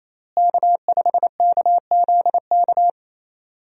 Morse code, heard as a keyed tone, about 28 wpm, 700 Hz. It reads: K5KZK
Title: USA Call Signs - 5 Characters - Repeated 3x - 28wpm